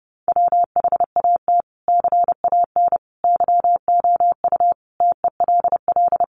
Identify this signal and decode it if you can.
Morse code, heard as a keyed tone, about 28 wpm, 700 Hz. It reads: WHAT CAN YOU TELL
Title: Sets of 5 Words Encore 30wpm